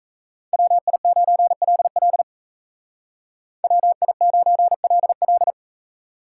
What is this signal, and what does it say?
Morse code, heard as a keyed tone, about 33 wpm, 700 Hz. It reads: WI9LL WI9LL
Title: Speed Racing - USA Call Signs 28wpm